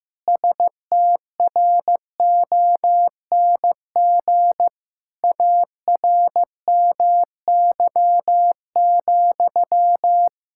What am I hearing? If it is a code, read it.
Morse code, heard as a keyed tone, about 15 wpm, 700 Hz. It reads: STRONG ARMY,
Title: Sentences from Top 1000 Words - Review Part 5 - 15wpm